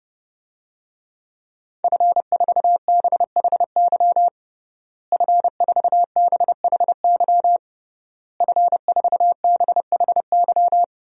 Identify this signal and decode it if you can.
Morse code, heard as a keyed tone, about 30 wpm, 700 Hz. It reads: F4BHY F4BHY F4BHY
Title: DXCC-100 Call Signs - 5 Characters - Repeated 3x - 30wpm